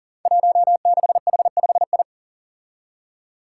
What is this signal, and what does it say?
Morse code, heard as a keyed tone, about 40 wpm, 700 Hz. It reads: 16H5I
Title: Five Challenging Letter-Number Combinations 40wpm